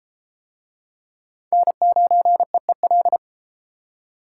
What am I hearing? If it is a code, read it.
Morse code, heard as a keyed tone, about 32 wpm, 700 Hz. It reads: N9EEL